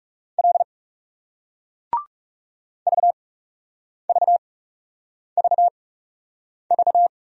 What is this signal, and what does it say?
Morse code, heard as a keyed tone, about 40 wpm, spaced as for 8 wpm, 700 Hz. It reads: L VVVV